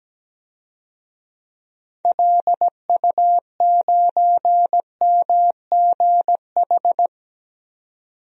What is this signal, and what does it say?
Morse code, heard as a keyed tone, about 17 wpm, 700 Hz. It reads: LU9MGH
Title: DXCC-100 Call Signs - 6 Characters - 17wpm